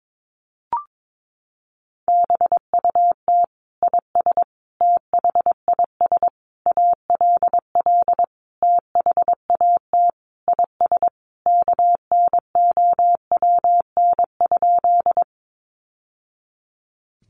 Morse code, heard as a keyed tone, about 22 wpm, 700 Hz. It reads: BUT IS THIS ALL THAT IS KNOWN?
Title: Sentences from Top 600 Words 22wpm